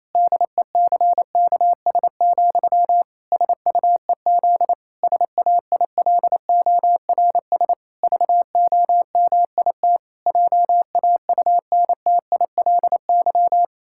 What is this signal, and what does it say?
Morse code, heard as a keyed tone, about 27 wpm, 700 Hz. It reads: DECKS, SUEZ SAILORS VOMIT JAUNTILY
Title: Pangram Challenge v2 - 28wpm